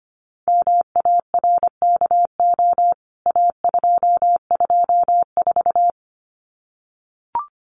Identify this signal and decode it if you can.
Morse code, heard as a keyed tone, about 25 wpm, 700 Hz. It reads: MARKO A224
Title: CWT Exchange 25wpm